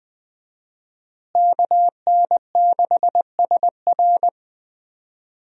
Morse code, heard as a keyed tone, about 20 wpm, 700 Hz. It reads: KN6SR
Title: USA Call Signs - 5 Characters - Repeated 2x - 20wpm